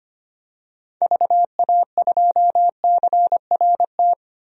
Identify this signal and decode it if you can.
Morse code, heard as a keyed tone, about 25 wpm, 700 Hz. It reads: VA2CRT
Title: DXCC-100 Call Signs - 6 Characters - 25wpm